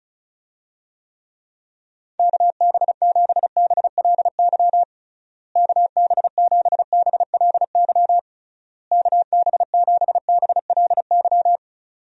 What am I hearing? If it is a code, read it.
Morse code, heard as a keyed tone, about 34 wpm, 700 Hz. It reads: KB7BLY KB7BLY KB7BLY
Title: USA Call Signs - 6 Characters - Repeated 3x - 35wpm